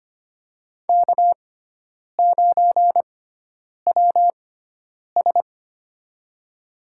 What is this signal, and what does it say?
Morse code, heard as a keyed tone, about 25 wpm, spaced as for 9 wpm, 700 Hz. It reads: K9WS
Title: USA Call Signs - 6x Character Spacing - 25wpm